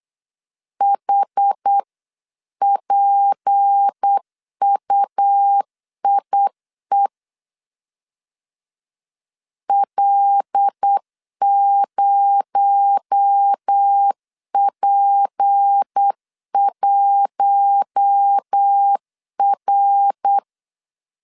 Morse code, heard as a keyed tone, about 8 wpm, 800 Hz. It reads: HPUIE L0P1R
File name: CW / DET / 1 6 1500C S B 8wpm 5142